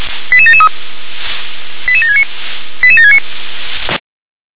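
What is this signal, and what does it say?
Five-tone selective call (ZVEI), heard as quick runs of five tones, tones of about 70 ms. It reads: 88773 88769 88769
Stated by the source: ZVEI five-tone call received by SDR.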